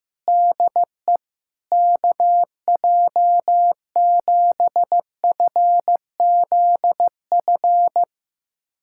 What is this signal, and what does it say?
Morse code, heard as a keyed tone, about 15 wpm, 700 Hz. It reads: DE KJ7FZF
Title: USA Callsign DE Callsign 15wpm